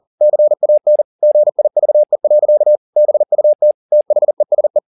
Morse code, heard as a keyed tone, about 40 wpm, 600 Hz. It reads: CAN GIVE. BUT THESE